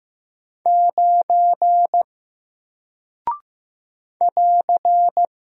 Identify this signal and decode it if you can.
Morse code, heard as a keyed tone, about 15 wpm, 700 Hz. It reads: 9 <AR>
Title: Single Character (Letter:Number:Punctuation:Prosign) 15wpm